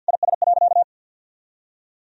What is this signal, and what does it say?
Morse code, heard as a keyed tone, about 50 wpm, 700 Hz. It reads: IS.
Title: Sentences from Top 1000 Words - Review Part 4 - 50wpm